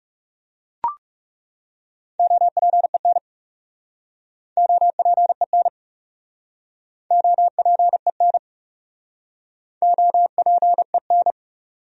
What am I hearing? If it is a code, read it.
Morse code, heard as a keyed tone, about 40 wpm, 700 Hz. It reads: OPEN OPEN OPEN OPEN